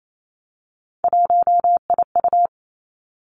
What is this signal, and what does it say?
Morse code, heard as a keyed tone, about 28 wpm, 700 Hz. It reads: 1IU
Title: Three Challenging Letter-Number Combinations 28wpm